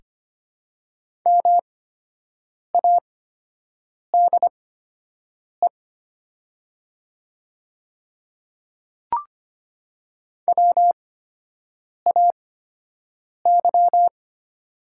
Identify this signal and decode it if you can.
Morse code, heard as a keyed tone, about 25 wpm, spaced as for 7 wpm, 700 Hz. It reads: MADE WAY
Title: 8x Character Spacing - Top 200 Words - 25wpm